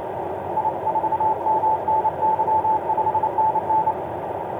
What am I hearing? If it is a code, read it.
Morse code, heard as a keyed tone, about 30 wpm, 870 Hz, with radio noise. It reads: I4MNY 5NN
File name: ox-dl2vfr i4mny etc 12m cw 22oct2010 1613